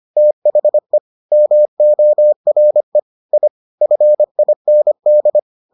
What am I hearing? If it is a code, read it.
Morse code, heard as a keyed tone, about 25 wpm, 600 Hz. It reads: THE MORE I FIND